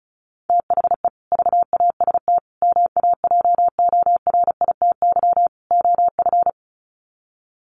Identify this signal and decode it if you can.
Morse code, heard as a keyed tone, about 34 wpm, 700 Hz. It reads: THE VAST MAJORITY OF
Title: Sets of 4 Words 35wpm